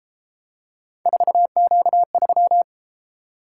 Morse code, heard as a keyed tone, about 31 wpm, 700 Hz. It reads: 4Q3